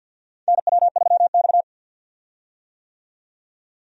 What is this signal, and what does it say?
Morse code, heard as a keyed tone, about 52 wpm, 700 Hz. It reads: NW3X
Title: USA Call Signs - 4 Characters - 50wpm